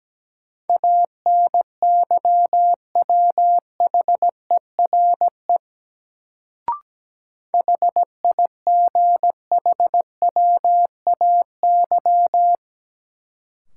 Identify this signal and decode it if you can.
Morse code, heard as a keyed tone, about 17 wpm, 700 Hz. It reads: ANYWHERE HIGHWAY
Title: Sentences from Top 1400 Words - New Word Drill Alt - 17wpm